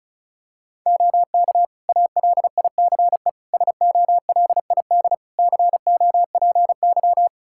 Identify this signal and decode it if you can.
Morse code, heard as a keyed tone, about 36 wpm, 700 Hz. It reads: OK ALICE SOLID COPY
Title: Practice QSO - Exchange 3 - More Info - 35wpm